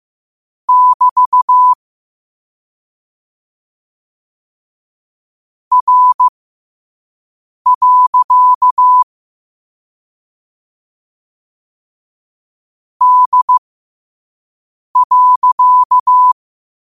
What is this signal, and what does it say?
Morse code, heard as a keyed tone, about 15 wpm, spaced as for 5 wpm, 1000 Hz. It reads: <BT> R. D.
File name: qotdfiles / 5wpm